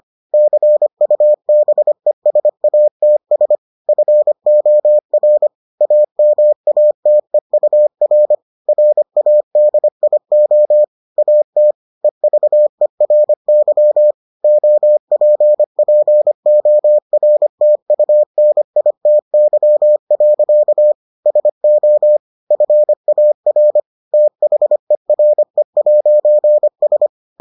Morse code, heard as a keyed tone, about 25 wpm, 600 Hz. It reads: CUBESATS FOR AMATEUR RADIO AT EVERY OPPORTUNITY. SO FAR THERE'S